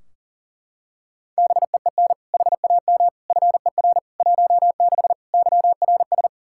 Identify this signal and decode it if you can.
Morse code, heard as a keyed tone, about 40 wpm, 700 Hz. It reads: BEEN HAM FER 16 YRS
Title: Practice QSO - Exchange 3 - More Info - 40wpm